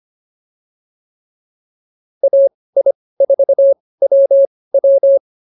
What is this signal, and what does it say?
Morse code, heard as a keyed tone, about 24 wpm, spaced as for 18 wpm, 550 Hz. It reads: AI4WW
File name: POTA203 18